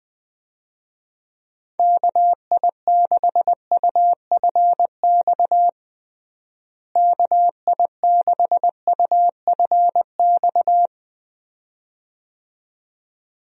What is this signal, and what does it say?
Morse code, heard as a keyed tone, about 21 wpm, 700 Hz. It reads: KI6UFX KI6UFX
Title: USA Call Signs - 6 Characters - Repeated 2x - 20wpm